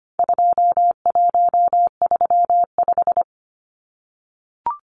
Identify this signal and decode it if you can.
Morse code, heard as a keyed tone, about 25 wpm, 700 Hz. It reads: 2135
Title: Numbers - 4-Digit 25wpm